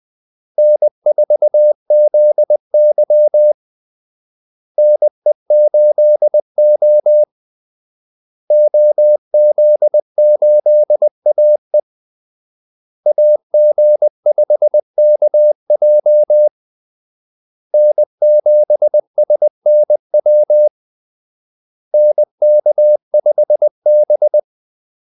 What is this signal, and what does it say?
Morse code, heard as a keyed tone, about 20 wpm, 600 Hz. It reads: N4ZY NE8O OZ8AE AG5KJ N7SNW NK5B
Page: Kaldesignaler 20 wpm